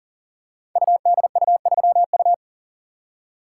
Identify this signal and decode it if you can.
Morse code, heard as a keyed tone, about 42 wpm, 700 Hz. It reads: UDU3U
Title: Code Groups - 5 Letter:Numbers - 40wpm